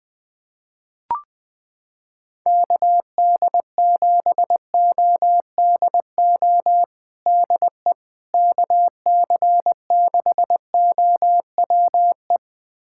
Morse code, heard as a keyed tone, about 20 wpm, 700 Hz. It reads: KD7ODO DE KC6OWE
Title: USA Callsign DE Callsign - 6 Characters - 20wpm